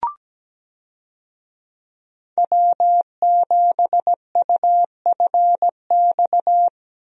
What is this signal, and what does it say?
Morse code, heard as a keyed tone, about 17 wpm, 700 Hz. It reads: W7UFX
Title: USA Call Signs - 5 Characters - Repeated 2x - 17wpm